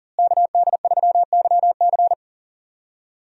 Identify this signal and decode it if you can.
Morse code, heard as a keyed tone, about 40 wpm, 700 Hz. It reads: KD3YC
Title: USA Call Signs - 5 Characters - Repeated 3x - 40wpm